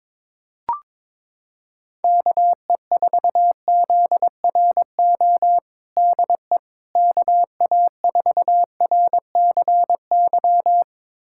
Morse code, heard as a keyed tone, about 22 wpm, 700 Hz. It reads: KE4ZRO DE KA4RCY
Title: USA Callsign DE Callsign 22wpm